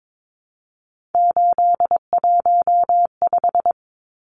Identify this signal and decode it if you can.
Morse code, heard as a keyed tone, about 22 wpm, 700 Hz. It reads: 815
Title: Numbers - 3-Digit 22wpm